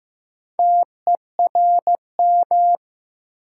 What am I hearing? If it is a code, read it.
Morse code, heard as a keyed tone, about 15 wpm, 700 Hz. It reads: TERM
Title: Sentences from Top 900 Words - New Word Drill - 15wpm